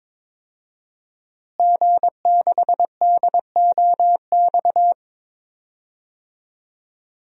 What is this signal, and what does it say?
Morse code, heard as a keyed tone, about 22 wpm, 700 Hz. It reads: G6DOX
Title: DXCC-100 Call Signs - 5 Characters - Repeated 3x - 22wpm